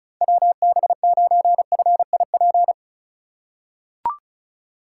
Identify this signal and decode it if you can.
Morse code, heard as a keyed tone, about 34 wpm, 700 Hz. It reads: WB9FIP
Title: K1USN SST QSO Practice - 35wpm